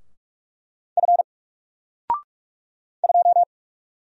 Morse code, heard as a keyed tone, about 44 wpm, 700 Hz. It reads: F 2
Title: Single Character (Letter:Number:Punctuation:Prosign) 45wpm